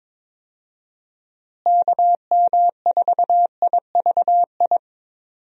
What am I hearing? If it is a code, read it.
Morse code, heard as a keyed tone, about 22 wpm, 700 Hz. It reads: KM4IVI
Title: USA Call Signs - 6 Characters - Repeated 2x - 22wpm